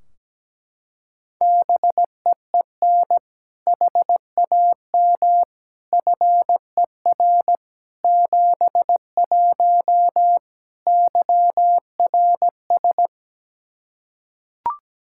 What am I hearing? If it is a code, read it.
Morse code, heard as a keyed tone, about 17 wpm, 700 Hz. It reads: BEEN HAM FER 71 YRS
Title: Practice QSO - Exchange 3 - More Info - 17wpm